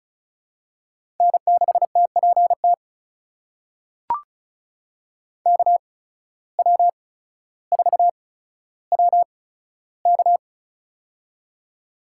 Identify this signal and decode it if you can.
Morse code, heard as a keyed tone, about 36 wpm, spaced as for 15 wpm, 700 Hz. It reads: N6TPT KW4WK
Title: USA Call Signs - 5 Characters - 8x Character Spacing - 35wpm